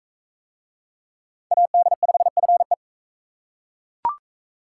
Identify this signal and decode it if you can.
Morse code, heard as a keyed tone, about 41 wpm, 700 Hz. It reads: AD5FE